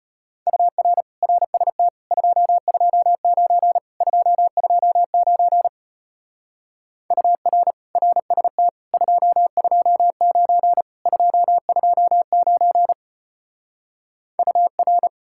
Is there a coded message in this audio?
Yes: Morse code from 0.5 to 15.1 s, heard as a keyed tone, about 33 wpm, 700 Hz. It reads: UR RST 229 229 UR RST 229 229 UR